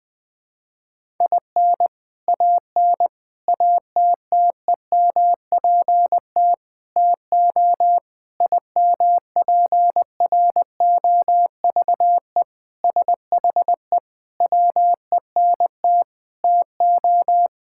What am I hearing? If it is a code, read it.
Morse code, heard as a keyed tone, about 20 wpm, 700 Hz. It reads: IN AN ATTEMPT TO IMPROVE SHE WENT TO
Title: Sentences with 4-Word Phrases - Spoken Beginning - 20wpm